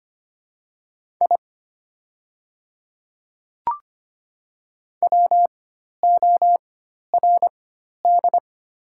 Morse code, heard as a keyed tone, about 25 wpm, spaced as for 12 wpm, 700 Hz. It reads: I WORD
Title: 4x Character Spacing - Top 100 Words - 25wpm